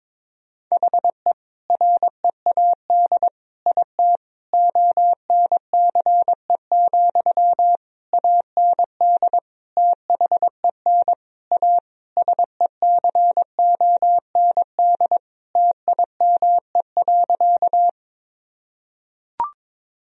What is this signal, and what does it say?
Morse code, heard as a keyed tone, about 22 wpm, 700 Hz. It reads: HE READ IT ONCE, AND THEN A SECOND TIME.
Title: Sentences from Top 1000 Words - Review Part 1 - 22wpm